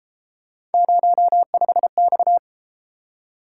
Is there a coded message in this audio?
Yes: Morse code from 0.7 to 2.4 s, heard as a keyed tone, about 33 wpm, 700 Hz. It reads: 05X